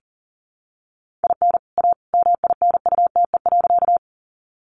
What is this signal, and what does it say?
Morse code, heard as a keyed tone, about 40 wpm, 700 Hz. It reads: IN A MINUTE.
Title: Sentences from Top 500 Words 40wpm